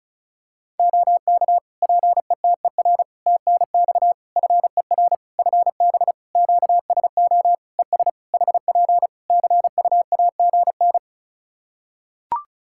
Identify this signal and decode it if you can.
Morse code, heard as a keyed tone, about 34 wpm, 700 Hz. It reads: OK PETER TNX FER FB QSO ES HP CUAGN
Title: K1USN SST QSO Practice 35wpm